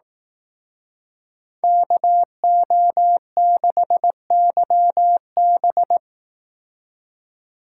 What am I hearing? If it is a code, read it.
Morse code, heard as a keyed tone, about 18 wpm, 700 Hz. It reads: KO6YB
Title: Speed Racing - USA 2x2 Call Signs 15wpm